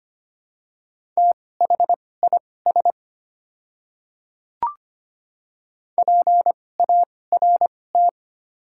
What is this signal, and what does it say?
Morse code, heard as a keyed tone, about 25 wpm, spaced as for 18 wpm, 690 Hz. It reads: THIS PART
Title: 2x Character Spacing - Top 300 Words - 25wpm